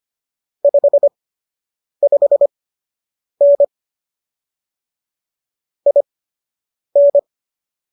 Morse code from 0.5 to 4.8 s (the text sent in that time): 55N